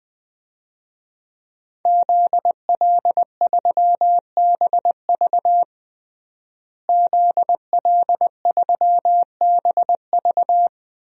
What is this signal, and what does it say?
Morse code, heard as a keyed tone, about 20 wpm, 700 Hz. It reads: ZL3BV ZL3BV
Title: DXCC-100 Call Signs - 5 Characters - Repeated 3x - 20wpm